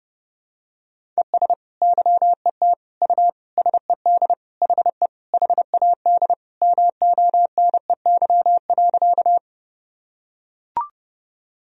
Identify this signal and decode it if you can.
Morse code, heard as a keyed tone, about 30 wpm, 700 Hz. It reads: ES YET U SED HE HAD MONEY.
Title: Sentences from Top 500 Words and Abbreviations 30wpm